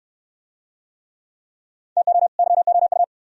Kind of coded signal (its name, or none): Morse code